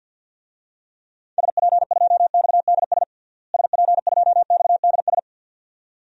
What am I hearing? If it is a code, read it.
Morse code, heard as a keyed tone, about 52 wpm, 700 Hz. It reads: SP2XDS SP2XDS
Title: DXCC-100 Call Signs - 6 Characters - Repeated 2x - 50wpm